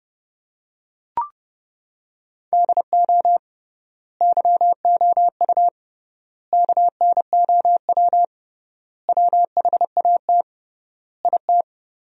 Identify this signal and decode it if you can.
Morse code, heard as a keyed tone, about 30 wpm, 700 Hz. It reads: DO YOU KNOW WHAT IT
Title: 3x Word-Spacing - Sentences from Top 100 Words - 30wpm